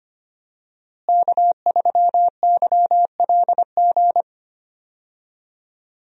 Morse code, heard as a keyed tone, about 25 wpm, 700 Hz. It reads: K3YLG
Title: USA Call Signs - 5 Characters - Repeated 2x - 25wpm